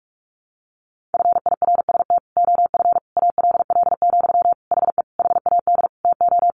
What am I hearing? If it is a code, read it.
Morse code, heard as a keyed tone, about 44 wpm, 700 Hz. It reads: FIRST OF ALL, HE HAD TO
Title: Sentences from Top 1200 Words 45wpm